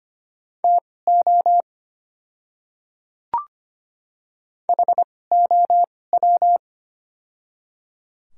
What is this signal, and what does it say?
Morse code, heard as a keyed tone, about 25 wpm, spaced as for 18 wpm, 700 Hz. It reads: TO HOW
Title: 2x Character Spacing - Top 100 Words - 25wpm